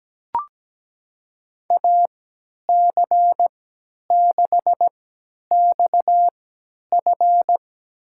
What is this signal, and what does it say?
Morse code, heard as a keyed tone, about 17 wpm, spaced as for 10 wpm, 700 Hz. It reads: AC6XF
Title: USA Call Signs - 5 Characters - 3x Character Spacing - 17wpm